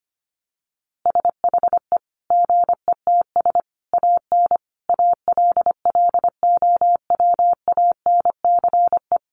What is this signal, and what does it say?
Morse code, heard as a keyed tone, about 25 wpm, 700 Hz. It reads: SHE GETS AN ALLOWANCE
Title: Easy Phrases 25wpm